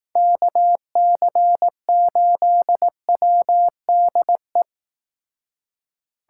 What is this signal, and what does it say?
Morse code, heard as a keyed tone, about 18 wpm, 700 Hz. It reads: KC8WDE
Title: Speed Racing - USA 2x3 Call Signs 15wpm